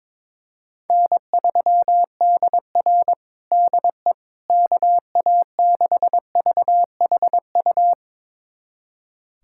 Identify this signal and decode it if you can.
Morse code, heard as a keyed tone, about 22 wpm, 700 Hz. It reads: N3DR DE KA6VHU
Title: USA Callsign DE Callsign 22wpm